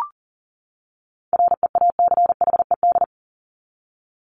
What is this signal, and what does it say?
Morse code, heard as a keyed tone, about 40 wpm, 700 Hz. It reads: REACHED
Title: ED Suffix - Top 100 Words with Base Words 40wpm